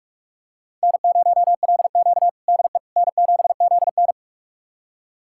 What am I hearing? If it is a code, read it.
Morse code, heard as a keyed tone, about 44 wpm, 690 Hz. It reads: N0LQ DE N7ZN
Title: USA Callsign DE Callsign - 4 Characters - 45wpm